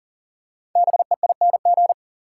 Morse code, heard as a keyed tone, about 40 wpm, 700 Hz. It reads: BEING
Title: Five Letter Words - 40wpm